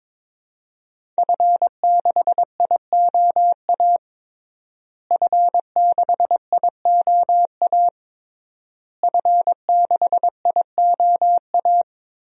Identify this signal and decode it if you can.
Morse code, heard as a keyed tone, about 22 wpm, 700 Hz. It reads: F6IOA F6IOA F6IOA
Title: DXCC-100 Call Signs - 5 Characters - Repeated 3x - 22wpm